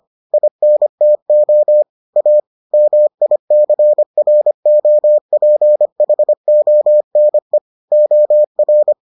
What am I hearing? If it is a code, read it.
Morse code, heard as a keyed tone, about 25 wpm, 600 Hz. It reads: INTO A MICROPHONE OR